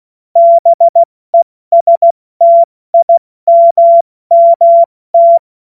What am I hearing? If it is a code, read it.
Morse code, heard as a keyed tone, about 16 wpm, spaced as for 14 wpm, 680 Hz. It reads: BESTIMMT
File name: Morsekurs/Witze/Witz 12 Dezember 2023 70 F80